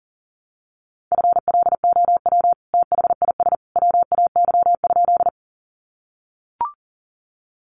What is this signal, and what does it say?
Morse code, heard as a keyed tone, about 40 wpm, 700 Hz. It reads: FLOW THIS WAY?
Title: Sentences from Top 1200 Words 40wpm